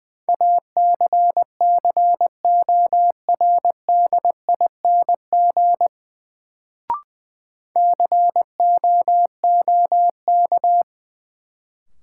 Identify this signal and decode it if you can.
Morse code, heard as a keyed tone, about 20 wpm, 700 Hz. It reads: ACCORDING COOK
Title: Sentences from Top 700 Words - New Word Drill - 20wpm